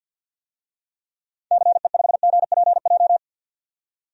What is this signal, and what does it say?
Morse code, heard as a keyed tone, about 52 wpm, 700 Hz. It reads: CE5GPJ
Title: DXCC-100 Call Signs - 6 Characters - 50wpm